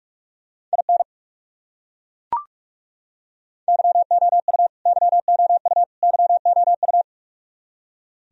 Morse code, heard as a keyed tone, about 42 wpm, 700 Hz. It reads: IN YOU YOU YOU
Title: Top 100 Words Repeated 45wpm